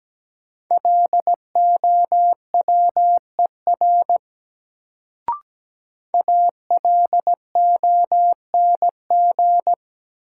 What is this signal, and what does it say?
Morse code, heard as a keyed tone, about 17 wpm, 700 Hz. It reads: LOWER ALONG
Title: Sentences from Top 700 Words - New Word Drill - 17wpm